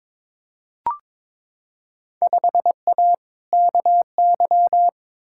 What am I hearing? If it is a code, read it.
Morse code, heard as a keyed tone, about 22 wpm, 700 Hz. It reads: 5A KY
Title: Field Day Exchanges - 22wpm